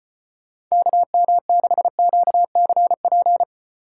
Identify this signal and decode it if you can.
Morse code, heard as a keyed tone, about 32 wpm, 700 Hz. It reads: KM6QCP